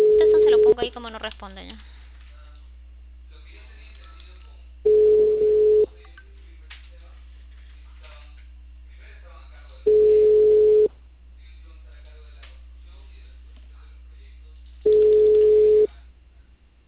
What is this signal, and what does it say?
Ringback tone (Europe, 425 Hz), with speech.